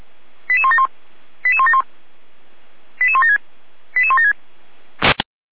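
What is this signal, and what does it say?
Five-tone selective call (ZVEI), heard as quick runs of five tones, tones of about 70 ms. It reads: 88171 88171 88176 88176
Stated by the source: ZVEI five-tone selective call 88413 received on an SDR.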